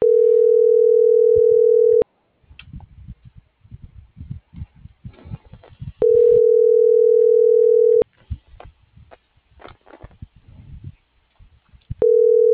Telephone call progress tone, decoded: ringback tone

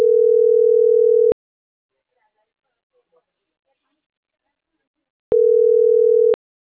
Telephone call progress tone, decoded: ringback tone

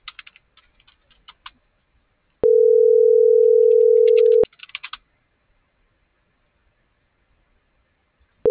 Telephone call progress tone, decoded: ringback tone

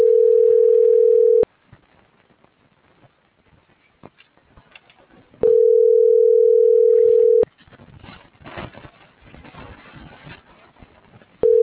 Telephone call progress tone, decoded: ringback tone

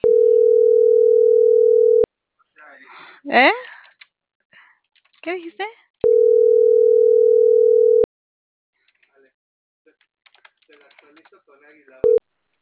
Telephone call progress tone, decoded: ringback tone

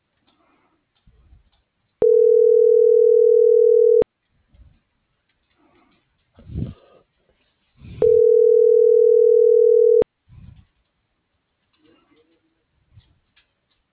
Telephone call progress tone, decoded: ringback tone

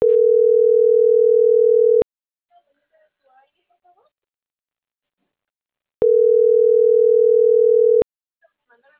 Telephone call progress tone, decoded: ringback tone

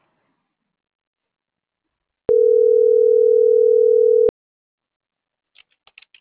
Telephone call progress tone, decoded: ringback tone

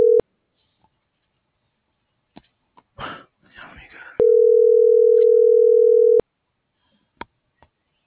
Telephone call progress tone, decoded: ringback tone

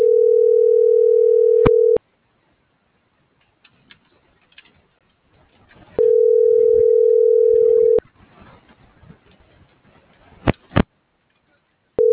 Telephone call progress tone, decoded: ringback tone

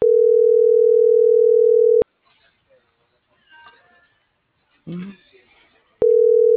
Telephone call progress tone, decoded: ringback tone

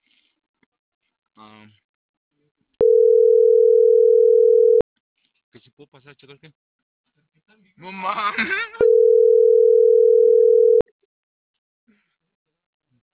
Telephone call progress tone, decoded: ringback tone

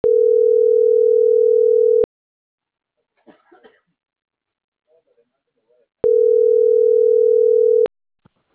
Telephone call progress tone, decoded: ringback tone

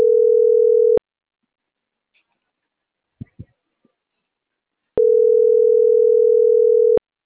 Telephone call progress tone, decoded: ringback tone

